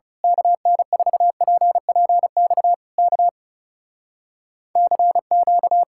Morse code, heard as a keyed tone, about 28 wpm, 700 Hz. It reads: KN4PPX K CQ
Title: Speed-Racing Common QSO Phrases - 30wpm